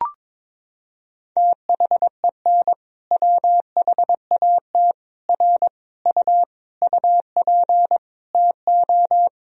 Morse code, heard as a keyed tone, about 22 wpm, 700 Hz. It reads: THEN WHAT R U UP TO
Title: Sentences from Top 100 Words and Abbreviations 22wpm